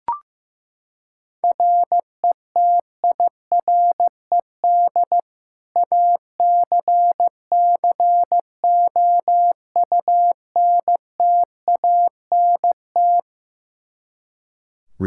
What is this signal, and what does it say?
Morse code, heard as a keyed tone, about 15 wpm, 700 Hz. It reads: RETIRED ACCOUNTANT
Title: Practice QSO - Exchange 3 - More Info - 15wpm